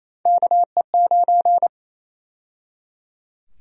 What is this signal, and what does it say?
Morse code, heard as a keyed tone, about 28 wpm, 700 Hz. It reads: KE9
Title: USA Call Sign Prefixes - 28wpm